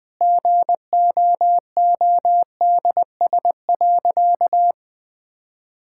Morse code, heard as a keyed tone, about 20 wpm, 700 Hz. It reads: GOODS.
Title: Sentences from Top 1000 Words - Review Part 6 - 20wpm